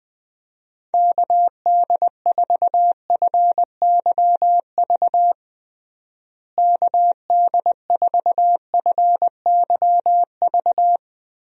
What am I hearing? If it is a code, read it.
Morse code, heard as a keyed tone, about 21 wpm, 700 Hz. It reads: KD4FYV KD4FYV
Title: USA Call Signs - 6 Characters - Repeated 3x - 20wpm